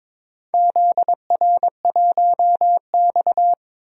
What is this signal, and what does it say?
Morse code, heard as a keyed tone, about 21 wpm, 700 Hz. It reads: ZR1X
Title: DXCC-100 Call Signs - 4 Characters - Repeated 2x - 22wpm